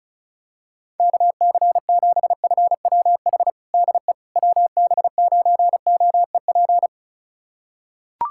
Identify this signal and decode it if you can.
Morse code, heard as a keyed tone, about 34 wpm, 700 Hz. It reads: KC7FWH DE WB9OEP
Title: USA Callsign DE Callsign - 6 Characters - 35wpm